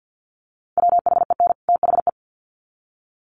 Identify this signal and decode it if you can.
Morse code, heard as a keyed tone, about 50 wpm, 700 Hz. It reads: WHEN THE
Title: Sets of 2 Words 50wpm